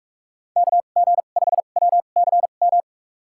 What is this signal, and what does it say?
Morse code, heard as a keyed tone, about 44 wpm, spaced as for 32 wpm, 700 Hz. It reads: KG5WCM
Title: USA Call Signs - 6 Characters - 2x Character Spacing - 45wpm